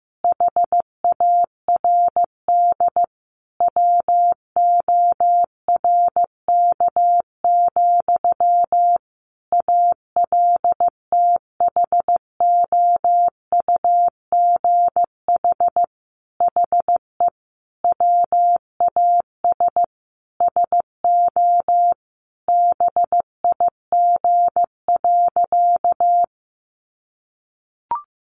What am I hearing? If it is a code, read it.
Morse code, heard as a keyed tone, about 15 wpm, 700 Hz. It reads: HARD WORK, ALTHOUGH HE WAS SO BIG.
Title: Sentences from Top 600 Words 15wpm